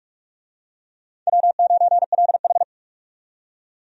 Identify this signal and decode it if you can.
Morse code, heard as a keyed tone, about 46 wpm, 700 Hz. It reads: W9LH
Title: USA Call Signs - 4 Characters - 45wpm